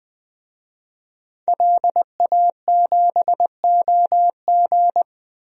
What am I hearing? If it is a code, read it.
Morse code, heard as a keyed tone, about 20 wpm, 700 Hz. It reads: LA7OG
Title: DXCC-100 Call Signs - 5 Characters - Repeated 3x - 20wpm